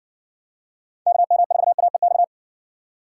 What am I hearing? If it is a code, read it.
Morse code, heard as a keyed tone, about 60 wpm, 700 Hz. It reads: KM4AEX